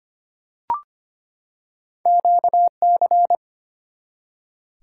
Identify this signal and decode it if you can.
Morse code, heard as a keyed tone, about 25 wpm, 700 Hz. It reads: QC
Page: morse-code-ninja-practice-25wpm